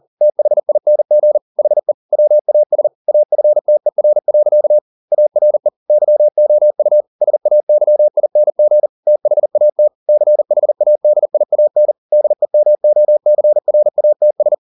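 Morse code, heard as a keyed tone, about 40 wpm, 600 Hz. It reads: THING HE WAS AFTER. ARE YOU SAYING THAT CHADIAN DEMOCRATS
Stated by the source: Morse code at 40 wpm, practice text, CW med Gnister.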